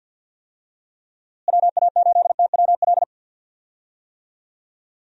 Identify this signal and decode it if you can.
Morse code, heard as a keyed tone, about 49 wpm, 700 Hz. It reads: WA8TWL
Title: USA Call Signs - 6 Characters - Repeated 3x - 50wpm